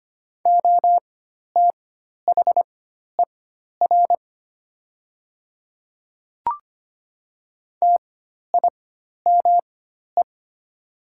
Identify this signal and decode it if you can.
Morse code, heard as a keyed tone, about 25 wpm, spaced as for 12 wpm, 700 Hz. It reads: OTHER TIME
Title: 4x Character Spacing - Top 100 Words - 25wpm